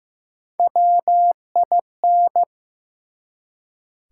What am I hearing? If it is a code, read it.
Morse code, heard as a keyed tone, about 15 wpm, 700 Hz. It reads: WIN
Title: Three Letter Words - 15wpm